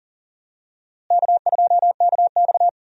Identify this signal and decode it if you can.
Morse code, heard as a keyed tone, about 40 wpm, 700 Hz. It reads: K2KX
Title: USA Call Signs - 4 Characters - Repeated 3x - 40wpm